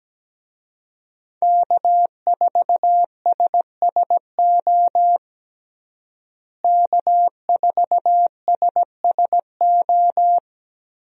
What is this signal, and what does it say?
Morse code, heard as a keyed tone, about 17 wpm, 700 Hz. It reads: K4SSO K4SSO
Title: USA Call Signs - 5 Characters - Repeated 2x - 17wpm